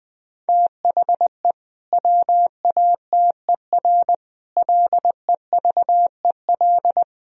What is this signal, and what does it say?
Morse code, heard as a keyed tone, about 20 wpm, 700 Hz. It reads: THE WATER LEVEL
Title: Sentences from Top 1000 Words - Review Part 4 - 20wpm